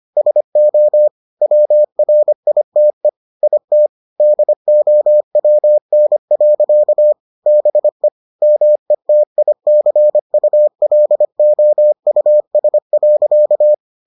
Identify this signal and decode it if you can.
Morse code, heard as a keyed tone, about 25 wpm, 600 Hz. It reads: SO WRITE IT DOWN. BE METICULOUS.